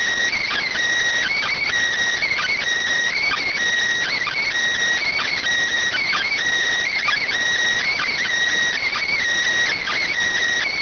Emergency Alert System (SAME) tones (header bursts), heard as harsh buzzing bursts. It is heard from 0.3 to 10.1 s.